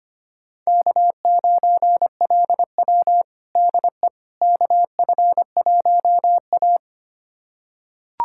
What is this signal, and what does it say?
Morse code, heard as a keyed tone, about 25 wpm, 700 Hz. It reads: K9LW DE KF1A
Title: USA Callsign DE Callsign - 4 Characters - 25wpm